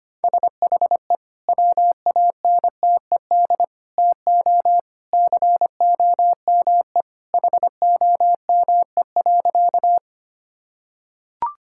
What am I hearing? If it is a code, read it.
Morse code, heard as a keyed tone, about 25 wpm, 700 Hz. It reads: SHE WANTED TO COME HOME.
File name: practice / 25wpm / Sentences with 2-Word Phrases - 25wpm